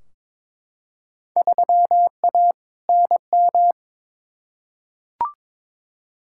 Morse code, heard as a keyed tone, about 22 wpm, 700 Hz. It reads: 3A NM
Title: Field Day Exchanges Encore - 22wpm